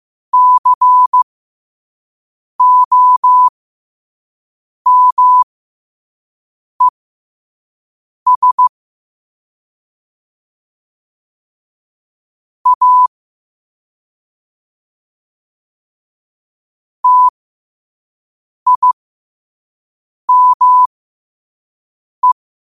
Morse code, heard as a keyed tone, about 15 wpm, spaced as for 5 wpm, 1000 Hz. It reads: COMES A TIME
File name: qotdfiles / 5wpm